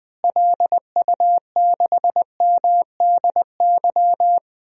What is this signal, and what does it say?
Morse code, heard as a keyed tone, about 21 wpm, 700 Hz. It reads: LU6MDY
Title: DXCC-100 Call Signs - 6 Characters - Repeated 3x - 20wpm